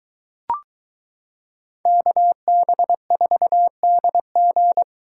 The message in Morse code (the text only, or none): KB4DG